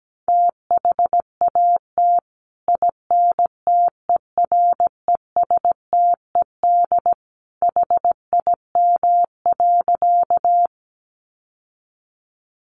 Morse code, heard as a keyed tone, about 17 wpm, 700 Hz. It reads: THAT INTERESTED HIM.
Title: Sentences from Top 900 Words 17wpm